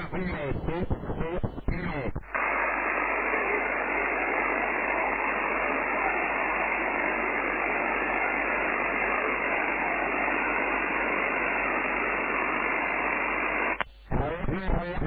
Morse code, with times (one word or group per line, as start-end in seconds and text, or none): none